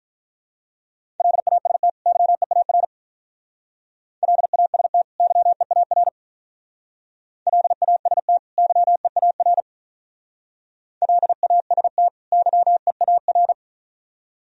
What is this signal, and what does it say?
Morse code, heard as a keyed tone, about 45 wpm, 700 Hz. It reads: LAST YEAR LAST YEAR LAST YEAR LAST YEAR